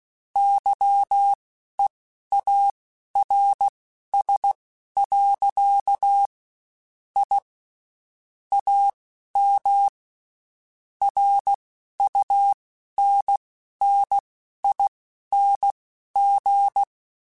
Morse code, heard as a keyed tone, about 16 wpm, spaced as for 12 wpm, 800 Hz. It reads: YEARS. I AM RUNNING